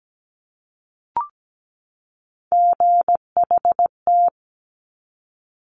Morse code, heard as a keyed tone, about 17 wpm, 700 Hz. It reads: GHT
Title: Top Three Letter Combinations in Words 17wpm